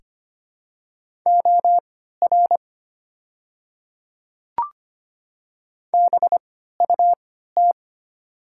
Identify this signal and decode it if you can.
Morse code, heard as a keyed tone, about 25 wpm, spaced as for 14 wpm, 700 Hz. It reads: OR BUT
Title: 3x Character Spacing - Top 100 Words - 25wpm